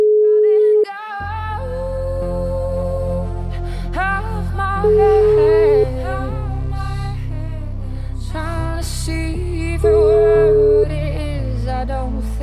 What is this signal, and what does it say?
Ringback tone (Europe, 425 Hz).